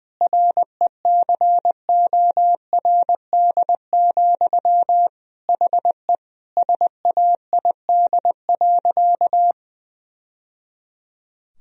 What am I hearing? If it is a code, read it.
Morse code, heard as a keyed tone, about 20 wpm, 700 Hz. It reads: RECORD, HE SAID.
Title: Sentences from Top 1000 Words - Review Part 3 - 20wpm